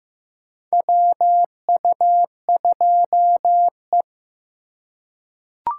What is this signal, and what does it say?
Morse code, heard as a keyed tone, about 15 wpm, 700 Hz. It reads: WU2E
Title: USA Call Signs - 4x Character Spacing - 15wpm